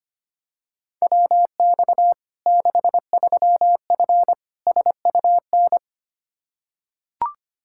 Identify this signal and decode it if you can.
Morse code, heard as a keyed tone, about 25 wpm, 700 Hz. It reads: WX 63F SUN
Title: Practice QSO - Exchange 2 - Info - 25wpm